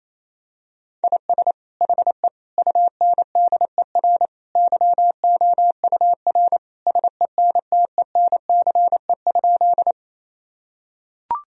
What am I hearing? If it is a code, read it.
Morse code, heard as a keyed tone, about 28 wpm, 700 Hz. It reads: IS HE UNDER YOUR SENTENCE?
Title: Sentences from Top 1000 Words - Review Part 1 - 28wpm